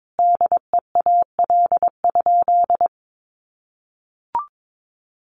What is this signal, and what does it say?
Morse code, heard as a keyed tone, about 22 wpm, 700 Hz. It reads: DEAL?
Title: Sentences from Top 1400 Words 22wpm